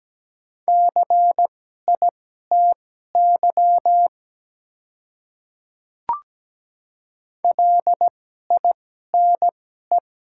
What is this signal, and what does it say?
Morse code, heard as a keyed tone, about 17 wpm, spaced as for 12 wpm, 700 Hz. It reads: CITY LINE
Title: 2x Character Spacing - Top 300 Words - 17wpm